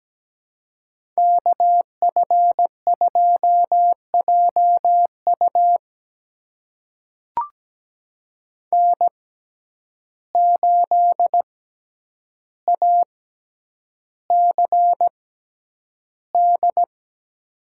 Morse code, heard as a keyed tone, about 17 wpm, spaced as for 9 wpm, 700 Hz. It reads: KF2JU N8ACD
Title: USA Call Signs - 5 Characters - 6x Character Spacing - 17wpm